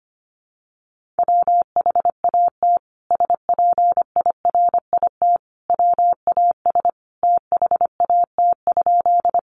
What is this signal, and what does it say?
Morse code, heard as a keyed tone, about 25 wpm, 700 Hz. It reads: WHAT SPIRIT WAS THAT?
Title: Sentences from Top 1400 Words 25wpm